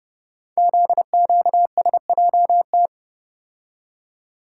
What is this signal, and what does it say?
Morse code, heard as a keyed tone, about 30 wpm, 700 Hz. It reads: ZQSJT
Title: Code Groups - 5 Letters - 30wpm